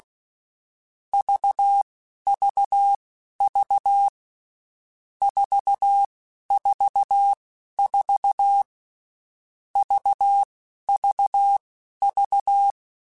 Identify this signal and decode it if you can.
Morse code, heard as a keyed tone, about 16 wpm, spaced as for 11 wpm, 800 Hz. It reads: VVV 444 VVV
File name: Groups@10